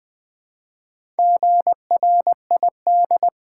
Morse code, heard as a keyed tone, about 20 wpm, 700 Hz. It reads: GRID
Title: Build A Word - 1 Letter At A Time - 4-Letter Words - 20wpm